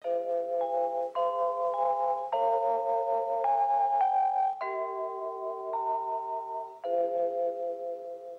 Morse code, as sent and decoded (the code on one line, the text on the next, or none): none